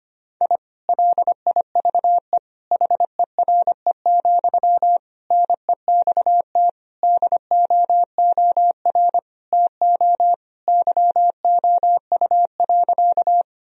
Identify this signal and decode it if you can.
Morse code, heard as a keyed tone, about 25 wpm, 700 Hz. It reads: I LIVE HERE, NEXT DOOR TO YOU.
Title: Sentences from Top 1000 Words - Review Part 1 - 25wpm